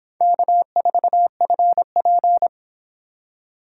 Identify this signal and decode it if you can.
Morse code, heard as a keyed tone, about 25 wpm, 700 Hz. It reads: K4FP